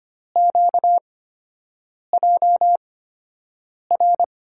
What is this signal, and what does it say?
Morse code, heard as a keyed tone, about 25 wpm, spaced as for 7 wpm, 700 Hz. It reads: QJR
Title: USA Call Signs - 8x Character Spacing - 25wpm